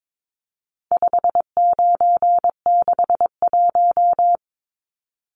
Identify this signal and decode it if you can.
Morse code, heard as a keyed tone, about 22 wpm, 700 Hz. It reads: 5961
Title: Numbers - 4-Digit 22wpm